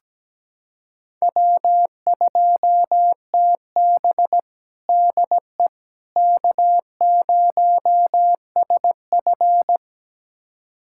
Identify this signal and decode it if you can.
Morse code, heard as a keyed tone, about 17 wpm, 700 Hz. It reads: W2TB DE K0SF
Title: USA Callsign DE Callsign - 4 Characters - 17wpm